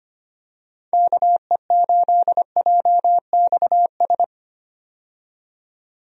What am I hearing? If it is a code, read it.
Morse code, heard as a keyed tone, about 25 wpm, 700 Hz. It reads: KE8JXS
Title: USA Call Signs - 6 Characters - 25wpm